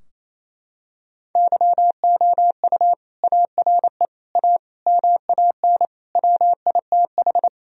Morse code, heard as a keyed tone, about 28 wpm, 700 Hz. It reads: YOU ARE A MAN WITH
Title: Sentences from Top 1000 Words - Review Part 3 - 28wpm